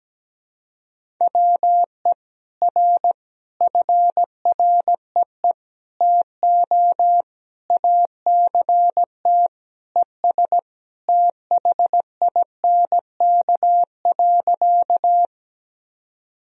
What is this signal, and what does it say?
Morse code, heard as a keyed tone, about 17 wpm, 700 Hz. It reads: WE R FREE TO ACT ES THINK.
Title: Sentences from Top 500 Words and Abbreviations 17wpm